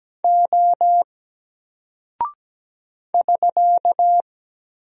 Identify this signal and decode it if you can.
Morse code, heard as a keyed tone, about 17 wpm, 700 Hz. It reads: O <SK>
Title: Single Character (Letter:Number:Punctuation:Prosign) Alt 17wpm